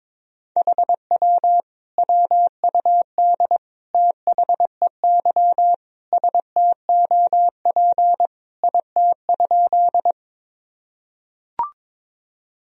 Morse code, heard as a keyed tone, about 22 wpm, 700 Hz. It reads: HW WUD THEY STOP IT?
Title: Sentences from Top 300 Words and Abbreviations 22wpm